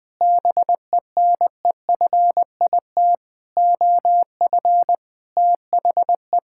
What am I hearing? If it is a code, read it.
Morse code, heard as a keyed tone, about 21 wpm, 700 Hz. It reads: BENEFIT OF THE
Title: Sets of 8 Words - 20wpm